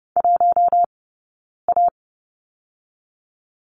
Morse code, heard as a keyed tone, about 30 wpm, spaced as for 9 wpm, 690 Hz. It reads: 1A
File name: Lesson 19d - Letter w - Two Characters 30wpm